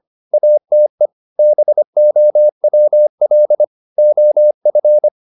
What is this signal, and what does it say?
Morse code, heard as a keyed tone, about 25 wpm, 600 Hz. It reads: ATE BOWL OF